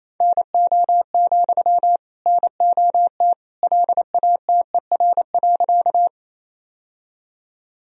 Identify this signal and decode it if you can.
Morse code, heard as a keyed tone, about 28 wpm, 700 Hz. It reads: NO, NOT LATER.
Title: Sentences from Top 1000 Words - Review Part 4 - 28wpm